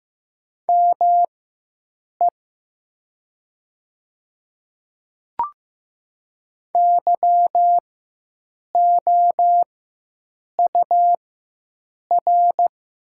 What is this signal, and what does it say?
Morse code, heard as a keyed tone, about 15 wpm, spaced as for 7 wpm, 700 Hz. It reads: ME YOUR
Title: 4x Character Spacing - Top 300 Words - 15wpm